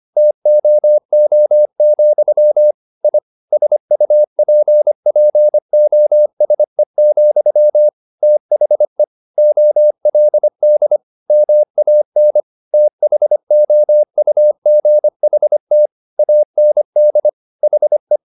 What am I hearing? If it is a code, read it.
Morse code, heard as a keyed tone, about 25 wpm, 600 Hz. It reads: TOO, I SUPPOSE, THE OLD MAN THOUGHT AND HE
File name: the old man and the sea part 1 25wpm